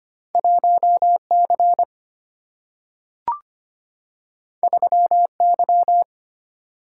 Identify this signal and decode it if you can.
Morse code, heard as a keyed tone, about 25 wpm, 700 Hz. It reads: 1C 3Y
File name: Two Letter-Number Combinations - 25wpm